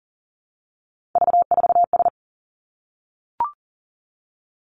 Morse code, heard as a keyed tone, about 42 wpm, 700 Hz. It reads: V4S
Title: Three Challenging Letter-Number Combinations 40wpm